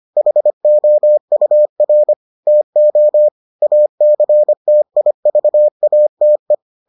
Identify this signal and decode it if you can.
Morse code, heard as a keyed tone, about 25 wpm, 600 Hz. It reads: HOUR TO ACTIVATE